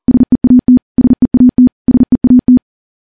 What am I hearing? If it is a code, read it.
Morse code, heard as a keyed tone, about 42 wpm, 250 Hz. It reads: SEAT SEAT SEAT
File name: morse / morse-training-40WPM